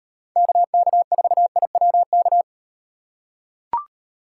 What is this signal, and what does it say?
Morse code, heard as a keyed tone, about 38 wpm, 700 Hz. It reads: KK4IWK